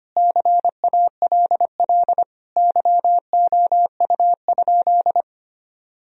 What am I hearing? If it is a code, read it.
Morse code, heard as a keyed tone, about 25 wpm, 700 Hz. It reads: CALL YOU?
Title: Sentences from Top 1000 Words - Review Part 3 - 25wpm